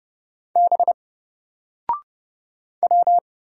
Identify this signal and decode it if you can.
Morse code, heard as a keyed tone, about 29 wpm, 700 Hz. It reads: B W
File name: Single-Letter-Number-30wpm